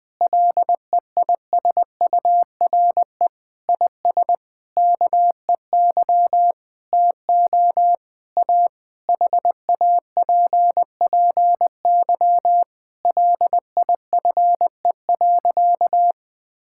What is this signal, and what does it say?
Morse code, heard as a keyed tone, about 21 wpm, 700 Hz. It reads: LEISURE IS KEY TO A HAPPY LIFE.
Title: Difficult Sentences with Hints - 20wpm